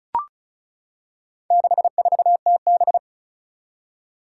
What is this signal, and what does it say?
Morse code, heard as a keyed tone, about 36 wpm, 700 Hz. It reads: 64TB
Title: Four Challenging Letter-Number Combinations 35wpm